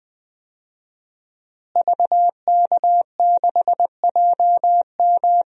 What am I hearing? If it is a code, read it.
Morse code, heard as a keyed tone, about 20 wpm, 700 Hz. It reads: VK6JM
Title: DXCC-100 Call Signs - 5 Characters - Repeated 2x - 20wpm